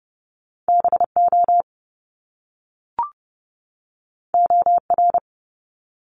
Morse code, heard as a keyed tone, about 30 wpm, 700 Hz. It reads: BO OR
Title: Top Two Letter Combinations in Words 30wpm